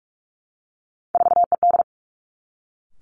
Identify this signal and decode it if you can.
Morse code, heard as a keyed tone, about 46 wpm, 700 Hz. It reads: 4ED
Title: Three Challenging Letter-Number Combinations 45wpm